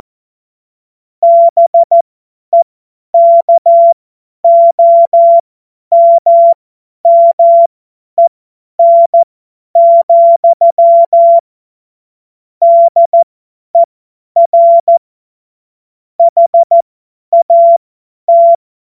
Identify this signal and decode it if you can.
Morse code, heard as a keyed tone, about 14 wpm, spaced as for 10 wpm, 680 Hz. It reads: BEKOMMEN, DER HAT
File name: Morsekurs / Witze / Witz 26 März 2024 50 F70